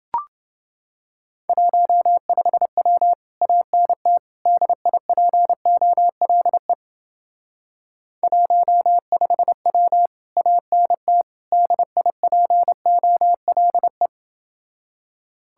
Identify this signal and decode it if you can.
Morse code, heard as a keyed tone, about 28 wpm, 700 Hz. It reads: 15W ANT DIPOLE 15W ANT DIPOLE